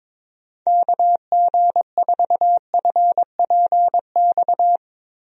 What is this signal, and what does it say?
Morse code, heard as a keyed tone, about 22 wpm, 700 Hz. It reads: KG4FPX
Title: USA Call Signs - 6 Characters - Repeated 3x - 22wpm